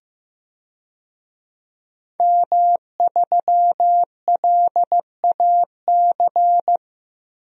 Morse code, heard as a keyed tone, about 15 wpm, 700 Hz. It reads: M3LAC
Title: DXCC-100 Call Signs - 5 Characters - Repeated 3x - 15wpm